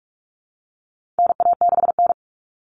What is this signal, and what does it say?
Morse code, heard as a keyed tone, about 44 wpm, 700 Hz. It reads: NA6N
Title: Common Cut Numbers - 4-Digit 45wpm